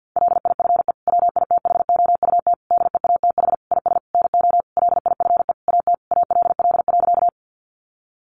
Morse code, heard as a keyed tone, about 50 wpm, 700 Hz. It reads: LIFE WITHOUT DEATH IS NO LIFE AT ALL.
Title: Sentences from Top 1100 Words 50wpm